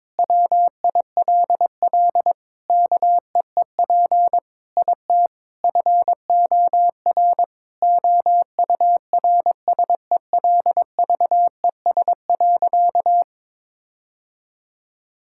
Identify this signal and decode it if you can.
Morse code, heard as a keyed tone, about 22 wpm, 700 Hz. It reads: WILL KEEP IT FOR OURSELVES.
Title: Sentences from Top 1600 Words 22wpm